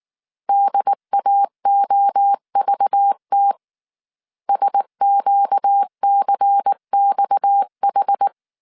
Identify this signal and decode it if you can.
Morse code, heard as a keyed tone, about 19 wpm, 800 Hz. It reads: DAOVT SQCXH
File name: CW / DET / 1 6 1500C S C 18wpm 2356